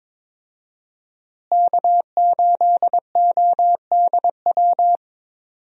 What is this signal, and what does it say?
Morse code, heard as a keyed tone, about 21 wpm, 700 Hz. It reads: K8ODW
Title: USA Call Signs - 5 Characters - Repeated 3x - 22wpm